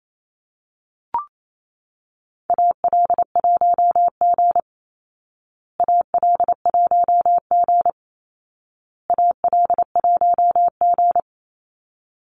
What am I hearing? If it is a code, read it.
Morse code, heard as a keyed tone, about 28 wpm, 700 Hz. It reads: AL1G AL1G AL1G
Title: Call Signs Repeated 2x - 28wpm